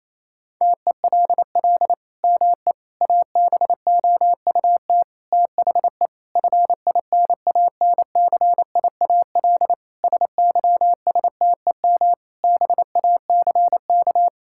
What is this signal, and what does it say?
Morse code, heard as a keyed tone, about 27 wpm, 700 Hz. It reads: TELL ME ABOUT THE FINANCIAL SYSTEM BACK
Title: Sentences from Top 1000 Words - Review Part 5 - 28wpm